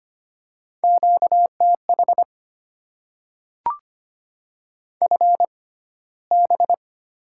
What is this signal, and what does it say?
Morse code, heard as a keyed tone, about 25 wpm, spaced as for 15 wpm, 700 Hz. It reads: QTH FB
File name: Top 20 QSO Elements - 6x Character Spacing - 25wpm